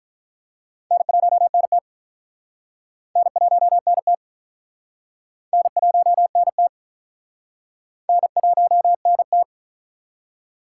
Morse code, heard as a keyed tone, about 42 wpm, 700 Hz. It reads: N1NT N1NT N1NT N1NT
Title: Speed Racing - USA 1x2 Call Signs 35wpm